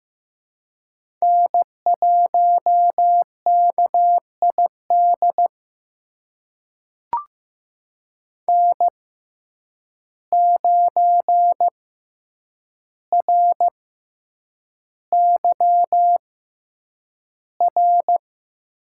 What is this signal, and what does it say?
Morse code, heard as a keyed tone, about 15 wpm, spaced as for 8 wpm, 700 Hz. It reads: N1KID N9RYR
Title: USA Call Signs - 6x Character Spacing - 15wpm